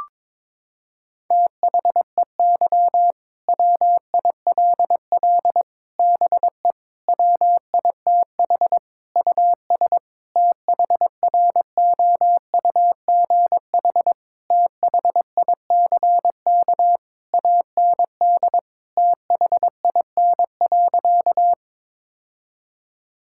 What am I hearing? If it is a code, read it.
Morse code, heard as a keyed tone, about 22 wpm, 700 Hz. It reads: THEY WILL BE WITH US THROUGH THICK AND THIN.
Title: Sentences from Top 1600 Words 22wpm